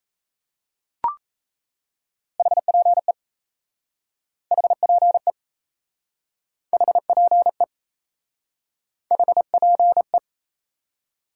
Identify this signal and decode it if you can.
Morse code, heard as a keyed tone, about 38 wpm, 700 Hz. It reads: HPE HPE HPE HPE